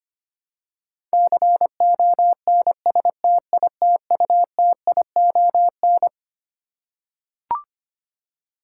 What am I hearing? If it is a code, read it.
Morse code, heard as a keyed tone, about 25 wpm, 700 Hz. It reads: CONSTITUTION
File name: Twelve Letter Words - 25wpm